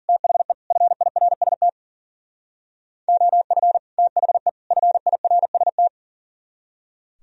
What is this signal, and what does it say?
Morse code, heard as a keyed tone, about 52 wpm, 700 Hz. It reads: THE FIRST OF THE FIRST